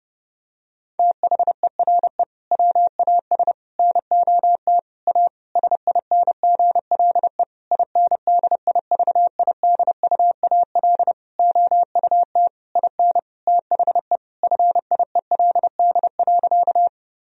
Morse code, heard as a keyed tone, about 30 wpm, 700 Hz. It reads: THERE WAS NOT A SINGLE INDIVIDUAL OUT IN THE FIELD.
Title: Sentences from Top 1000 Words - Review Part 4 - 30wpm